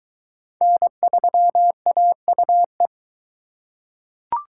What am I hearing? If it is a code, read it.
Morse code, heard as a keyed tone, about 21 wpm, 700 Hz. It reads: N3AUE